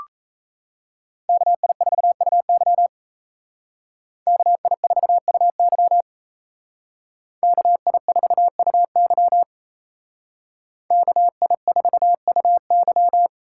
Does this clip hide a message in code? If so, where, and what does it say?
Yes: Morse code from 1.3 to 13.3 s, heard as a keyed tone, about 36 wpm, 700 Hz. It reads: KI4UY KI4UY KI4UY KI4UY